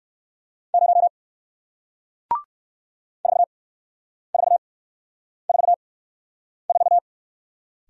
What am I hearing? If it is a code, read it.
Morse code, heard as a keyed tone, about 58 wpm, spaced as for 9 wpm, 700 Hz. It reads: 0 4444